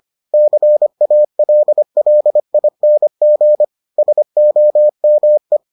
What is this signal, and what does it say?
Morse code, heard as a keyed tone, about 25 wpm, 600 Hz. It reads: CALLING SOME